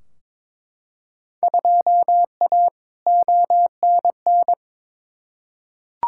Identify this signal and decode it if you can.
Morse code, heard as a keyed tone, about 22 wpm, 700 Hz. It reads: 2A ONN
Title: Field Day Exchanges Encore - 22wpm